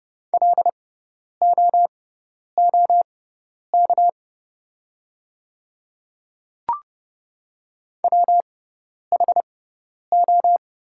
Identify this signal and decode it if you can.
Morse code, heard as a keyed tone, about 30 wpm, spaced as for 10 wpm, 700 Hz. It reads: LOOK WHO
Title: 6x Character Spacing - Top 100 Words - 30wpm